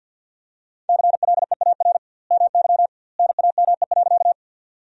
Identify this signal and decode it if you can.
Morse code, heard as a keyed tone, about 51 wpm, 700 Hz. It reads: CLEAR MY NAME.
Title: 2x Word-Spacing - Sentences from Top 500 Words - 50wpm